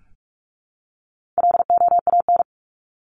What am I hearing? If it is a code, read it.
Morse code, heard as a keyed tone, about 44 wpm, 700 Hz. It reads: LOAN
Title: Top 1501-1750 Words 45wpm